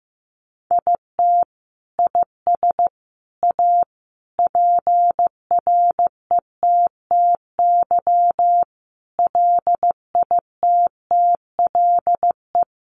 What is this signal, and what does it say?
Morse code, heard as a keyed tone, about 15 wpm, 700 Hz. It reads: IT IS A PRETTY LITTLE
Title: Sentences from Top 1000 Words 15wpm